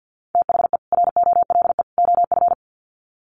Morse code, heard as a keyed tone, about 50 wpm, 700 Hz. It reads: THE ROLE OF
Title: Sets of 3 Words 50wpm